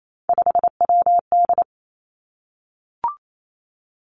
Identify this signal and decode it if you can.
Morse code, heard as a keyed tone, about 28 wpm, 700 Hz. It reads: 5WD
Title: Three Challenging Letter-Number Combinations 28wpm